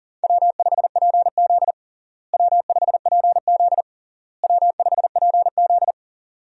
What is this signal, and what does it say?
Morse code, heard as a keyed tone, about 40 wpm, 700 Hz. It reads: W5PZ W5PZ W5PZ
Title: USA Call Signs - 4 Characters - Repeated 3x - 40wpm